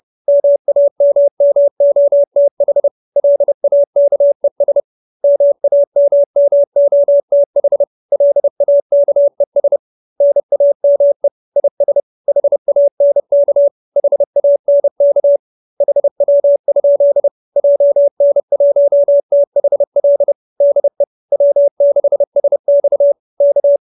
Morse code, heard as a keyed tone, about 30 wpm, 570 Hz. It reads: MAMMOTH LAKES MAMMOTH LAKES NAME IS HANK HANK HW? JN1THL DE W6SX K